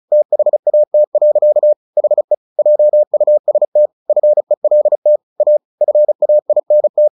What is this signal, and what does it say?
Morse code, heard as a keyed tone, about 34 wpm, 600 Hz. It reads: THAT. HE JUST FELT A FAINT
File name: the old man and the sea 2 35wpm